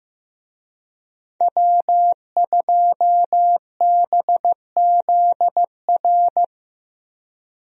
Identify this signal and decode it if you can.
Morse code, heard as a keyed tone, about 15 wpm, 700 Hz. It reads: W2BZR
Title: USA Call Signs - 5 Characters - Repeated 3x - 15wpm